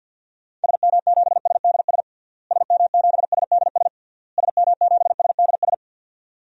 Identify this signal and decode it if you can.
Morse code, heard as a keyed tone, about 51 wpm, 700 Hz. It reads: SM7SDS SM7SDS SM7SDS
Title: DXCC-100 Call Signs - 6 Characters - Repeated 3x - 50wpm